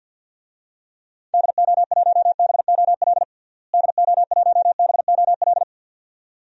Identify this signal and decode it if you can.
Morse code, heard as a keyed tone, about 47 wpm, 700 Hz. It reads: DO1BOL DO1BOL
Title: DXCC-100 Call Signs - 6 Characters - Repeated 2x - 50wpm